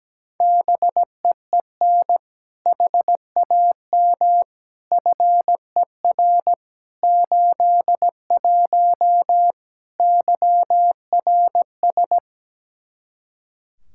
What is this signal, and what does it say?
Morse code, heard as a keyed tone, about 17 wpm, 700 Hz. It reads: BEEN HAM FER 81 YRS
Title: Practice QSO - Exchange 3 - More Info - 17wpm